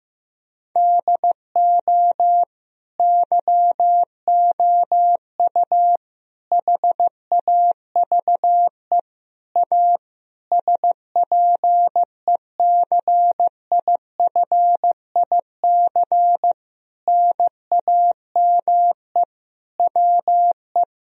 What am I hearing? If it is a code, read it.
Morse code, heard as a keyed tone, about 15 wpm, 700 Hz. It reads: DO YOU HAVE A SPECIFIC NAME WE
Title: Sentences from Top 1000 Words - Review Part 3 - 15wpm